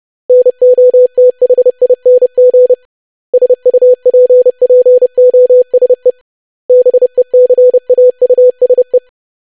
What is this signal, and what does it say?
Morse code, heard as a keyed tone, about 30 wpm, 500 Hz. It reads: NOTHING SUPPOSE BECAUSE
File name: lesson / web / basic500-25wpm